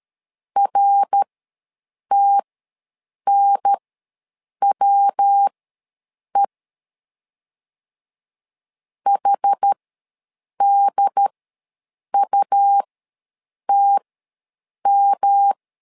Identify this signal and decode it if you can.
Morse code, heard as a keyed tone, about 13 wpm, spaced as for 7 wpm, 800 Hz. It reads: RTNWE HDUTM
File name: LEZIONI / 3 MWRO / 3 MWR ws 12wpm 1809